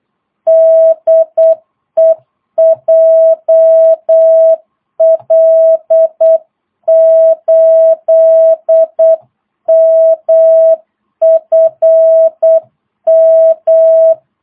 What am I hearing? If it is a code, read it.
Morse code, heard as a keyed tone, about 8 wpm, 660 Hz. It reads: DE JL8MFM
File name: jh8cbh / 2022cw